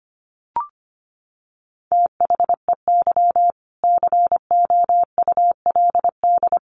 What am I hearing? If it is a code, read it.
Morse code, heard as a keyed tone, about 25 wpm, 700 Hz. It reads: THEY COULD
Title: Sentences from Top 900 Words 25wpm